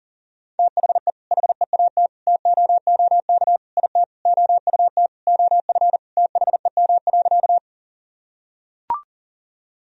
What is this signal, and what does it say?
Morse code, heard as a keyed tone, about 40 wpm, 700 Hz. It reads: THE HEAT TOOK IT OUT OF THEM.
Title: Sentences from Top 1000 Words - Review Part 3 - 40wpm